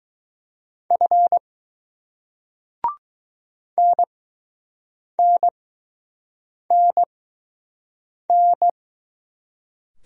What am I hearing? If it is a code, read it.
Morse code, heard as a keyed tone, about 20 wpm, spaced as for 6 wpm, 700 Hz. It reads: F NNNN